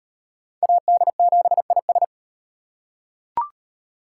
Morse code, heard as a keyed tone, about 34 wpm, 700 Hz. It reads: AD7IS